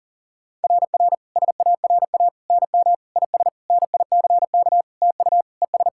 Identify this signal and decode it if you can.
Morse code, heard as a keyed tone, about 40 wpm, 690 Hz. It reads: RR SARA NM IS NICK TU ES
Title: SOTA QSO Practice 40wpm